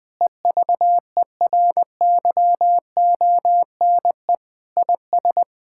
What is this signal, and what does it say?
Morse code, heard as a keyed tone, about 20 wpm, 700 Hz. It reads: EVERYONE IS
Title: Sentences with 5-Word Phrases - 20wpm